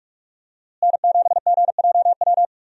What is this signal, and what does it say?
Morse code, heard as a keyed tone, about 48 wpm, 700 Hz. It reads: N7GJW
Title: USA Call Signs - 5 Characters - Repeated 2x - 45wpm